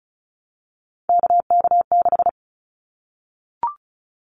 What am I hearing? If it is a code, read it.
Morse code, heard as a keyed tone, about 36 wpm, 700 Hz. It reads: KK6
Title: Callsign Prefixes for Top 25 DXCC - 35wpm